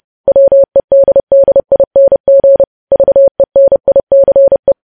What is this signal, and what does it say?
Morse code, heard as a keyed tone, about 30 wpm, 550 Hz. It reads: WEDDING VENICE
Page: News Headlines in Morse Code at 30 WPM